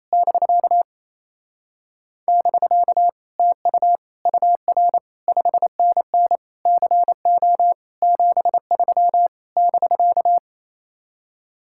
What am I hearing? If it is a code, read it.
Morse code, heard as a keyed tone, about 27 wpm, 700 Hz. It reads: <BK> <BK> TU UR 5NN CO 73 <BK>
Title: Speed-Racing Common QSO Phrases - 28wpm